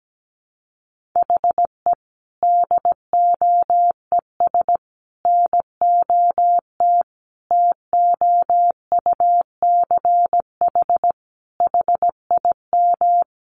Morse code, heard as a keyed tone, about 17 wpm, 700 Hz. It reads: HE DOES NOT TOUCH HIM
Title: Sentences from Top 1400 Words 17wpm